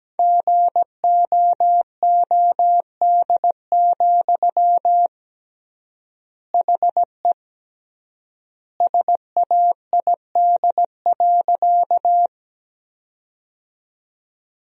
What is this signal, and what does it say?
Morse code, heard as a keyed tone, about 17 wpm, 700 Hz. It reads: GOOD, HE SAID.
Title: 3x Word-Spacing - Sentences from Top 200 Words - 17wpm